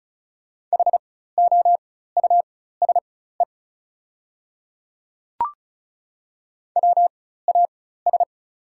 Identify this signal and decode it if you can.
Morse code, heard as a keyed tone, about 36 wpm, spaced as for 16 wpm, 690 Hz. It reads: HOUSE WAS
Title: 4x Character Spacing - Top 200 Words - 35wpm